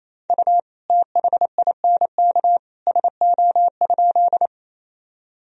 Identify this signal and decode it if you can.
Morse code, heard as a keyed tone, about 28 wpm, 700 Hz. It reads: U THINK SO?
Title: Sentences from Top 200 Words and Abbreviations 28wpm